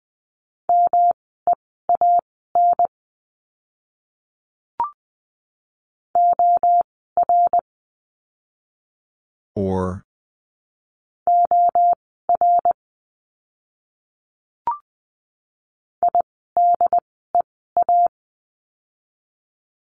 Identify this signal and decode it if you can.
Morse code, heard as a keyed tone, about 20 wpm, spaced as for 14 wpm, 700 Hz. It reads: MEAN OR OR IDEA
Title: 108 Lesson 18f - Letter m - Words 20wpm